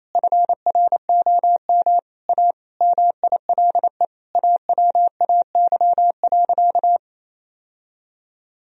Morse code, heard as a keyed tone, about 28 wpm, 700 Hz. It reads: FROM A MILE AWAY.
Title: Sentences from Top 1000 Words - Review Part 3 - 28wpm